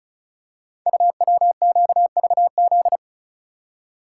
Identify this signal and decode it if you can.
Morse code, heard as a keyed tone, about 36 wpm, 700 Hz. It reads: UWQVZ
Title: Code Groups - 5 Letters - 35wpm